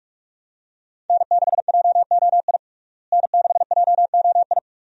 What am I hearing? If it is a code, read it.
Morse code, heard as a keyed tone, about 42 wpm, 700 Hz. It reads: N6JOI N6JOI
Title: USA Call Signs - 5 Characters - Repeated 2x - 45wpm